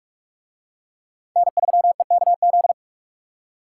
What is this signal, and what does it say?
Morse code, heard as a keyed tone, about 44 wpm, 700 Hz. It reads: N3EKZ